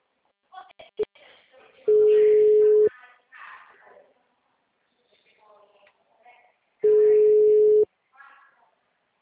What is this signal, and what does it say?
Ringback tone (Europe, 425 Hz), with speech.
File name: out- -2007- - - .63430